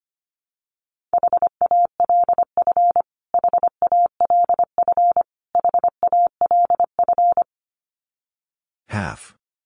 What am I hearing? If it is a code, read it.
Morse code, heard as a keyed tone, about 25 wpm, 700 Hz. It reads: HALF HALF HALF
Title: Top 500 Words Repeated 25wpm